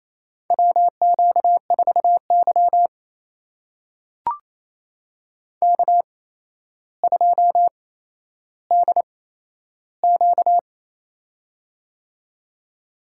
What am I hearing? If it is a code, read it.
Morse code, heard as a keyed tone, about 28 wpm, spaced as for 12 wpm, 700 Hz. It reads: WQ4Y K2DQ
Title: USA Call Signs - 4 Characters - 8x Character Spacing - 28wpm